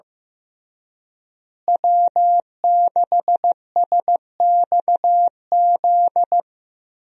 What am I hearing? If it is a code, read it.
Morse code, heard as a keyed tone, about 15 wpm, 700 Hz. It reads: W6SXZ
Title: USA Call Signs - 5 Characters - Repeated 3x - 15wpm